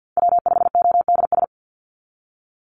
Morse code, heard as a keyed tone, about 50 wpm, 700 Hz. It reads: W5ODS
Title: NA SOTA Chaser Callsigns 50wpm